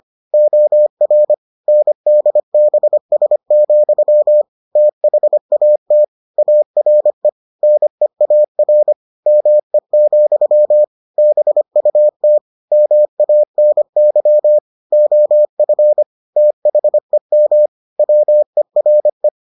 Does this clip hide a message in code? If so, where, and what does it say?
Yes: Morse code from 0.3 to 19.3 s, heard as a keyed tone, about 25 wpm, 600 Hz. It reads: OR NDBS, THAT ARE NEAR ME, BUT MANY OF THEM WERE